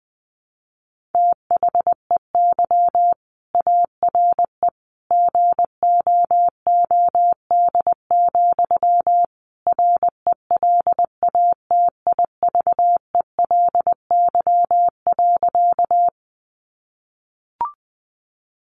Morse code, heard as a keyed tone, about 20 wpm, 700 Hz. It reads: THEY ARE GOOD, RELATIVELY.
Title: Sentences from Top 1100 Words 20wpm